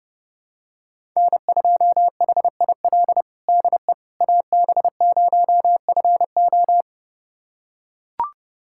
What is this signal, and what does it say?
Morse code, heard as a keyed tone, about 32 wpm, 700 Hz. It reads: N2HIL DE AB0FO
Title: USA Callsign DE Callsign - 5 Characters - 30wpm